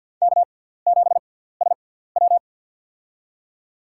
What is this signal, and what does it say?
Morse code, heard as a keyed tone, about 53 wpm, spaced as for 17 wpm, 700 Hz. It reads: K7SW
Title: USA Call Signs - 6x Character Spacing - 50wpm